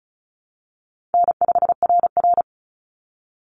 Morse code, heard as a keyed tone, about 34 wpm, 700 Hz. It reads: N5RR
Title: Easy Call Signs Repeated 3x - 35wpm